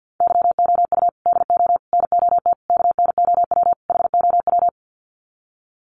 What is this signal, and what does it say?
Morse code, heard as a keyed tone, about 51 wpm, 700 Hz. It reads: YOU DO NOT KNOW HOW
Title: Sets of 5 Words Encore 50wpm